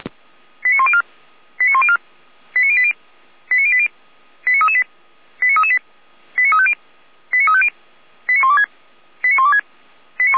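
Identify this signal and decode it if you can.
Five-tone selective call (ZVEI), heard as quick runs of five tones, tones of about 70 ms. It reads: 89104 89104 89080 89080 89338 89338 89350 89350 89126 89126
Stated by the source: ZVEI five-tone selective call received on an SDR.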